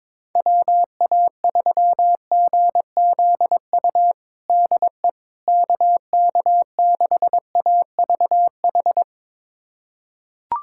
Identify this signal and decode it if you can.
Morse code, heard as a keyed tone, about 22 wpm, 700 Hz. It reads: WA3GZU DE KK6AVH
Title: USA Callsign DE Callsign - 6 Characters - 22wpm